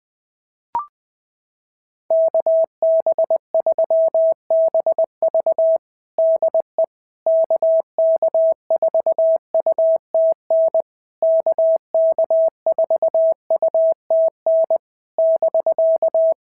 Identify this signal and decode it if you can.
Morse code, heard as a keyed tone, about 20 wpm, 650 Hz. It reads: KB3BV DE KK4UTN KK4UTN <BK>
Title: SOTA QSO Practice 20wpm